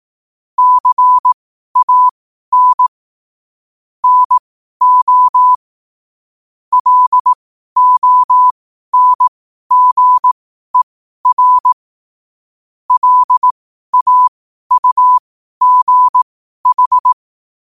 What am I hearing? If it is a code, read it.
Morse code, heard as a keyed tone, about 18 wpm, spaced as for 12 wpm, 1000 Hz. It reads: CAN NO LONGER LAUGH
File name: qotdfiles / 12wpm